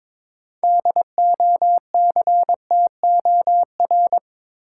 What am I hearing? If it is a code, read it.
Morse code, heard as a keyed tone, about 21 wpm, 700 Hz. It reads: DOCTOR
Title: Sentences from Top 1500 Words - New Word Drill Alt - 22wpm